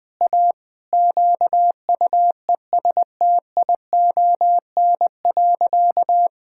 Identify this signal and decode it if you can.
Morse code, heard as a keyed tone, about 20 wpm, 700 Hz. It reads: A QUESTION.
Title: Sentences with 5-Word Phrases - 20wpm